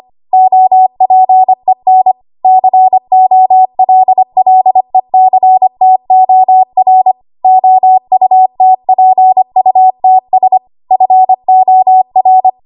Morse code, heard as a keyed tone, about 25 wpm, 750 Hz. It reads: OPEN COLLECTOR OUTPUTS FOR
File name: Archive / 25 WPM / 25WPM